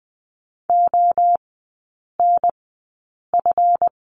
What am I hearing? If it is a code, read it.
Morse code, heard as a keyed tone, about 20 wpm, spaced as for 8 wpm, 700 Hz. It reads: ONF
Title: 138 Lesson 23e - Letter f - Three Characters 20wpm